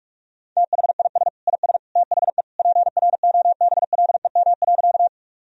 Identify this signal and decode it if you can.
Morse code, heard as a keyed tone, about 44 wpm, 700 Hz. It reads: THIS IS THE PROBLEM.
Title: Sentences from Top 1000 Words - Review Part 2 - 45wpm